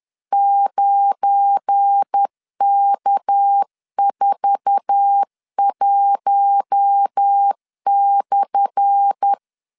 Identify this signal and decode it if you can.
Morse code, heard as a keyed tone, about 10 wpm, 800 Hz. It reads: 9K41/
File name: LEZIONI / 10 SIGNS / 10A SLASH 10wpm 2808